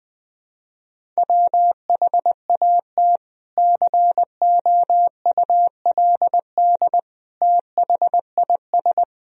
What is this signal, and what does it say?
Morse code, heard as a keyed tone, about 21 wpm, 700 Hz. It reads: WHAT COULD THIS
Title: Sentences from Top 1000 Words - Review Part 2 - 20wpm